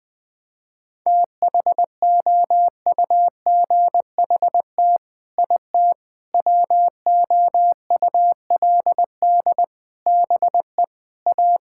Morse code, heard as a keyed tone, about 20 wpm, 700 Hz. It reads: THOUGHT IT WOULD BE A
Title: Sets of 8 Words - 20wpm